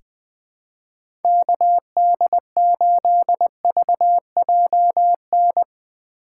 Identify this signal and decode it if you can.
Morse code, heard as a keyed tone, about 20 wpm, 700 Hz. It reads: KD8VJN
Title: USA Call Signs - 6 Characters - 6x Character Spacing - 20wpm